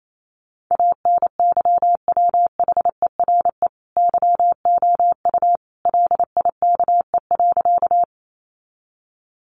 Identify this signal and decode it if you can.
Morse code, heard as a keyed tone, about 28 wpm, 700 Hz. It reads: ANYWHERE YOU LIKE.
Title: Sentences from Top 1400 Words 28wpm